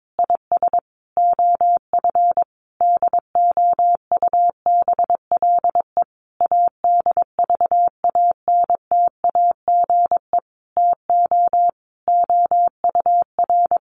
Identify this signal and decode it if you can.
Morse code, heard as a keyed tone, about 22 wpm, 700 Hz. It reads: IS OF DOUBLE ADVANTAGE TO OUR
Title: Sentences from Top 1400 Words 22wpm